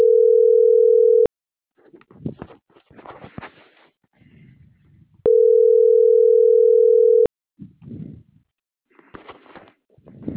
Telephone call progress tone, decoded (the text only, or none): ringback tone